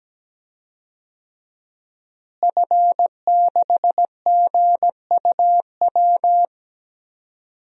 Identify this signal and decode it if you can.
Morse code, heard as a keyed tone, about 17 wpm, 700 Hz. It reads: F6GUW
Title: DXCC-100 Call Signs - 5 Characters - Repeated 2x - 17wpm